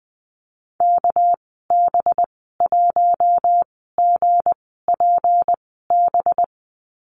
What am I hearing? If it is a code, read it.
Morse code, heard as a keyed tone, about 20 wpm, spaced as for 15 wpm, 700 Hz. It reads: KB1GPB
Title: 200 Lesson 32g - Letter k - Call Signs 20wpm